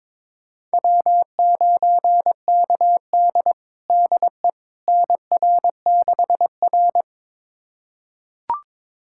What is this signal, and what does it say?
Morse code, heard as a keyed tone, about 22 wpm, 700 Hz. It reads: W9KD DE NR6R
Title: USA Callsign DE Callsign - 4 Characters - 22wpm